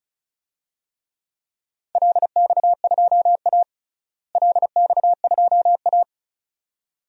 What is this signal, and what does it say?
Morse code, heard as a keyed tone, about 36 wpm, 700 Hz. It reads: LX2A LX2A
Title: DXCC-100 Call Signs - 4 Characters - Repeated 2x - 35wpm